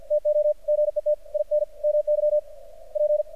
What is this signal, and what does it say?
Morse code, heard as a keyed tone, about 51 wpm, 600 Hz. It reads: TO GET IN MY P
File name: archived / CRSnet